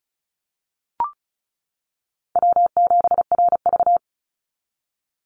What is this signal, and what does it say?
Morse code, heard as a keyed tone, about 34 wpm, 700 Hz. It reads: W7RV
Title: NA SOTA Chaser Callsigns 35wpm